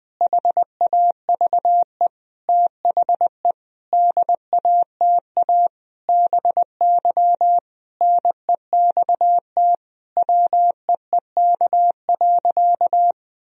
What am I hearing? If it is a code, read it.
Morse code, heard as a keyed tone, about 21 wpm, 700 Hz. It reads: HAVE THE DATA BY NEXT WEEK.
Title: Sentences from Top 1000 Words - Review Part 3 - 20wpm